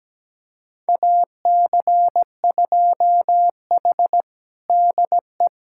Morse code, heard as a keyed tone, about 17 wpm, 700 Hz. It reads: AC2H DE
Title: USA Callsign DE Callsign - 4 Characters - 17wpm